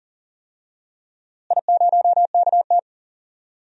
Morse code, heard as a keyed tone, about 40 wpm, 700 Hz. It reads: I0KT
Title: DXCC-100 Call Signs - 4 Characters - Repeated 3x - 40wpm